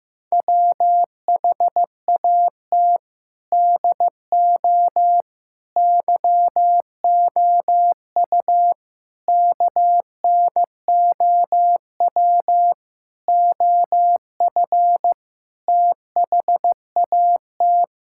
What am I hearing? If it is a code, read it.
Morse code, heard as a keyed tone, about 16 wpm, 700 Hz. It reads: WHAT DO YOU KNOW OF THAT
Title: Sentences from Top 1000 Words - Review Part 5 - 15wpm